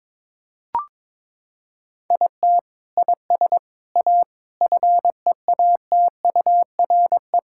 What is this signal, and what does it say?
Morse code, heard as a keyed tone, about 22 wpm, 700 Hz. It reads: IT IS A FEATURE
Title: Sentences from Top 1600 Words 22wpm